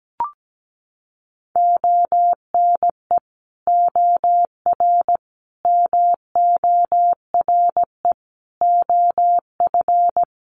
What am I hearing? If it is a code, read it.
Morse code, heard as a keyed tone, about 17 wpm, 700 Hz. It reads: ONE OR MORE OF
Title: Sets of 5 Words Encore 17wpm